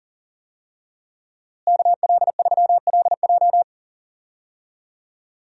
Morse code, heard as a keyed tone, about 38 wpm, 700 Hz. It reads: KL3LJ
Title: USA Call Signs - 5 Characters - Repeated 2x - 40wpm